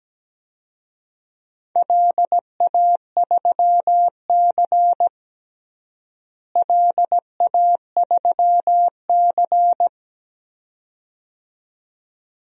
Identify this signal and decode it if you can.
Morse code, heard as a keyed tone, about 17 wpm, 700 Hz. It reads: LA3C LA3C
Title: DXCC-100 Call Signs - 4 Characters - Repeated 2x - 17wpm